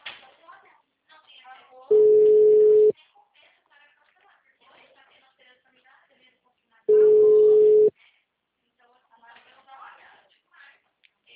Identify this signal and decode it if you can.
Ringback tone (Europe, 425 Hz), with speech.
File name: out- -2022- - - .72564